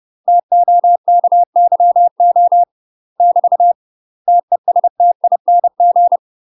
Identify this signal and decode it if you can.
Morse code, heard as a keyed tone, about 30 wpm, 700 Hz. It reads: TOKYO <BT> TESTING
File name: UK MASTER QSOs TRAINING / 30WPM / UK MASTER QSOs 0004